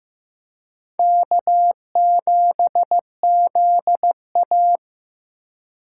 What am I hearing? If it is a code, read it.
Morse code, heard as a keyed tone, about 18 wpm, 700 Hz. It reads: K7ZA
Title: Speed Racing - USA 1x2 Call Signs 15wpm